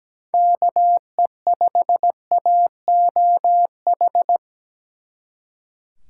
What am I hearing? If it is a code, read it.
Morse code, heard as a keyed tone, about 20 wpm, 700 Hz. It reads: KE5AOH
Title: Speed Racing - USA 2x3 Call Signs 17wpm